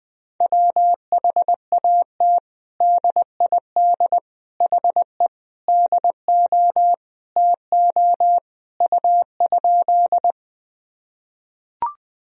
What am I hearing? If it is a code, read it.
Morse code, heard as a keyed tone, about 20 wpm, 700 Hz. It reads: WHAT DID HE DO TO U?
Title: Sentences from Top 100 Words and Abbreviations 20wpm